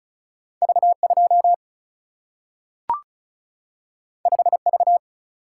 Morse code, heard as a keyed tone, about 36 wpm, 700 Hz. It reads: V 2 5 V
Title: Single-Letter Single-Number 35wpm